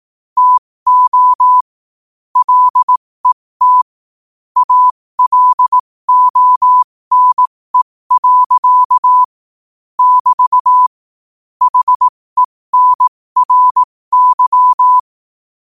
Morse code, heard as a keyed tone, about 18 wpm, spaced as for 15 wpm, 1000 Hz. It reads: TO LET ALONE. <BT> HENRY
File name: qotdfiles / 15wpm